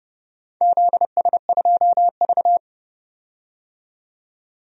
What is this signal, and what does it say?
Morse code, heard as a keyed tone, about 30 wpm, 700 Hz. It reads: ZS2V
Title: DXCC-100 Call Signs - 4 Characters - 30wpm